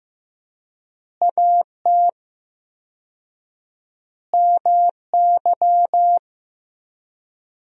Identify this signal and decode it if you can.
Morse code, heard as a keyed tone, about 15 wpm, 700 Hz. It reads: AT MY
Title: 4x Word-Spacing - Sentences from Top 200 Words - 15wpm